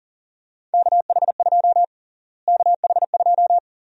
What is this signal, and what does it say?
Morse code, heard as a keyed tone, about 40 wpm, 700 Hz. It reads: KH2 KH2
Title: USA Call Sign Prefixes - Repeated 2x - 40wpm